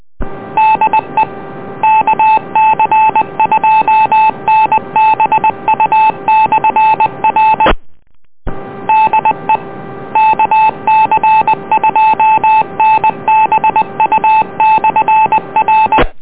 Morse code, heard as a keyed tone, about 20 wpm, 870 Hz. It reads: DE KC2NBU/R DE KC2NBU/R